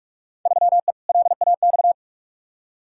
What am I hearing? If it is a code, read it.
Morse code, heard as a keyed tone, about 41 wpm, 700 Hz. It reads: 3E LAX
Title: Field Day Exchanges Encore - 45wpm